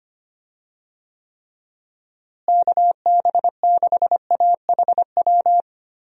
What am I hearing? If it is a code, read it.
Morse code, heard as a keyed tone, about 25 wpm, 700 Hz. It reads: KB6AHW
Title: USA Call Signs - 6 Characters - Repeated 3x - 25wpm